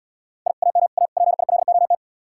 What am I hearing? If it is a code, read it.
Morse code, heard as a keyed tone, about 77 wpm, 700 Hz. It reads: I AM A PILOT